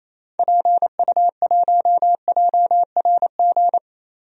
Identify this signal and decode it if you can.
Morse code, heard as a keyed tone, about 28 wpm, 700 Hz. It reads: PU1JRG
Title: DXCC-100 Call Signs - 6 Characters - Repeated 2x - 28wpm